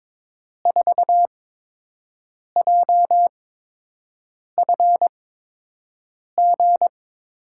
Morse code, heard as a keyed tone, about 23 wpm, spaced as for 6 wpm, 700 Hz. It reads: 4JFG
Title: USA Call Signs - 8x Character Spacing - 22wpm